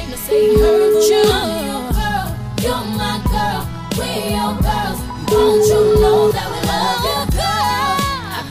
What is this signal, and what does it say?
Ringback tone (Europe, 425 Hz), with noise.